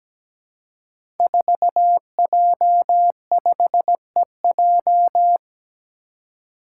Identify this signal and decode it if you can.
Morse code, heard as a keyed tone, about 17 wpm, 700 Hz. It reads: 4J5EJ
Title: Five Challenging Letter-Number Combinations 17wpm